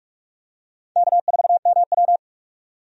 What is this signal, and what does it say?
Morse code, heard as a keyed tone, about 44 wpm, 700 Hz. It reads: K4MW
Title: USA Call Signs - 4 Characters - 45wpm